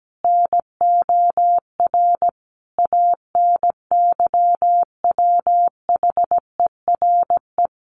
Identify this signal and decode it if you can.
Morse code, heard as a keyed tone, about 17 wpm, 700 Hz. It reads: NOR ANYWHERE
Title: Sentences from Top 1500 Words 17wpm